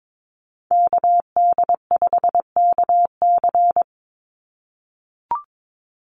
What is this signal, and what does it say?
Morse code, heard as a keyed tone, about 22 wpm, 700 Hz. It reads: KD5KC
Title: NA SOTA Chaser Callsigns 22wpm